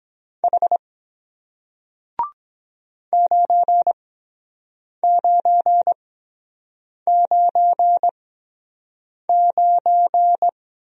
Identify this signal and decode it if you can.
Morse code, heard as a keyed tone, about 23 wpm, spaced as for 7 wpm, 700 Hz. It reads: H 9999